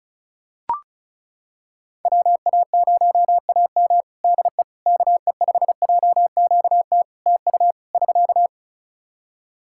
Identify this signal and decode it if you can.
Morse code, heard as a keyed tone, about 36 wpm, 700 Hz. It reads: WA0AM DE KE5JQT TU <SK>
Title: Practice QSO - Exchange 4 - Ending - 35wpm